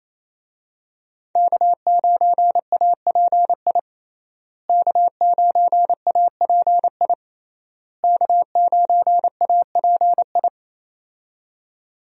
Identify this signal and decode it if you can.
Morse code, heard as a keyed tone, about 28 wpm, 700 Hz. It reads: K9API K9API K9API
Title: USA Call Signs - 5 Characters - Repeated 3x - 28wpm